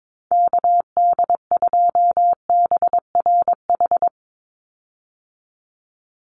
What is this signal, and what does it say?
Morse code, heard as a keyed tone, about 22 wpm, 700 Hz. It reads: KD2BRH
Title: USA Call Signs 22wpm